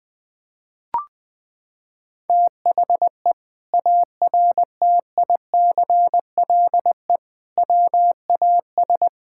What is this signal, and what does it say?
Morse code, heard as a keyed tone, about 20 wpm, 700 Hz. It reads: THE ARTICLE WAS
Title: Sentences from Top 1000 Words - Review Part 5 - 20wpm